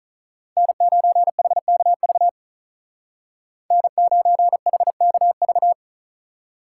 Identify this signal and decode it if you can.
Morse code, heard as a keyed tone, about 46 wpm, 700 Hz. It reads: N9HKV N9HKV